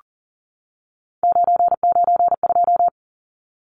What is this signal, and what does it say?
Morse code, heard as a keyed tone, about 40 wpm, 700 Hz. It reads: 992
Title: Numbers - 3-Digit 40wpm